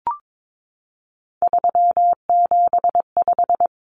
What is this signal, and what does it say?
Morse code, heard as a keyed tone, about 22 wpm, 700 Hz. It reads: 375
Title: Common Cut Numbers - 3-Digit 22wpm